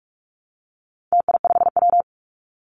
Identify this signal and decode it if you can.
Morse code, heard as a keyed tone, about 44 wpm, 700 Hz. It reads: TI5W
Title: Call Signs - 45wpm